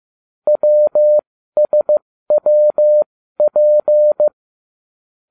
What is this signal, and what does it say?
Morse code, heard as a keyed tone, about 15 wpm, spaced as for 13 wpm, 600 Hz. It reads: WSWP
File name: cw / ubacw12 1